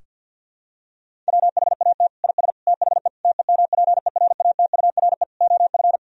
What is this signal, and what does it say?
Morse code, heard as a keyed tone, about 51 wpm, 700 Hz. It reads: WHAT IS THE TEMPERATURE OF
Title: Sentences from Top 1000 Words - Review Part 4 - 50wpm